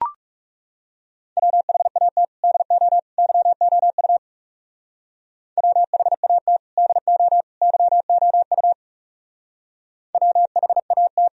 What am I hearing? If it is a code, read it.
Morse code, heard as a keyed tone, about 39 wpm, 700 Hz. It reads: WHAT DO YOU WHAT DO YOU WHAT